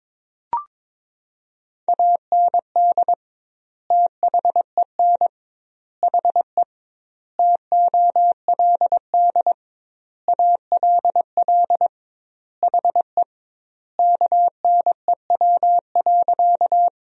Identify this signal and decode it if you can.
Morse code, heard as a keyed tone, about 22 wpm, 700 Hz. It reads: AND THEN HE TOLD ALL HE KNEW.
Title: 2x Word-Spacing - Sentences from Top 400 Words - 22wpm